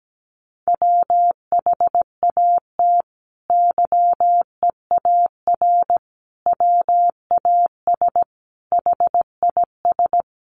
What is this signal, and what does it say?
Morse code, heard as a keyed tone, about 17 wpm, 700 Hz. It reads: WHAT YEAR WAS HIS
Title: Sentences from Top 800 Words 17wpm